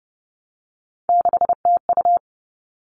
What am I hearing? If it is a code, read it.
Morse code, heard as a keyed tone, about 30 wpm, 700 Hz. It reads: 6TU
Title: All Cut Numbers - 3-Digit 30wpm